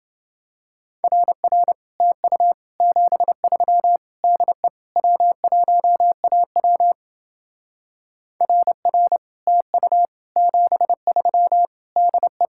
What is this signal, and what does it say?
Morse code, heard as a keyed tone, about 29 wpm, 700 Hz. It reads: RR TU 73 DE W1AW RR TU 73 DE